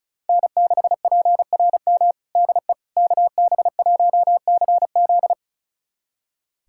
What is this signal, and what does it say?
Morse code, heard as a keyed tone, about 34 wpm, 700 Hz. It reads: N6PRM DE KB1CZ
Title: USA Callsign DE Callsign - 5 Characters - 35wpm